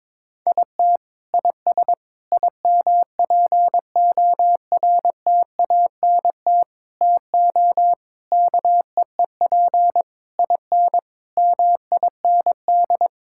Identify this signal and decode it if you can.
Morse code, heard as a keyed tone, about 22 wpm, 700 Hz. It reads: IT IS IMPORTANT TO KEEP IN MIND
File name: practice / 22wpm / Sets of 8 Words - 22wpm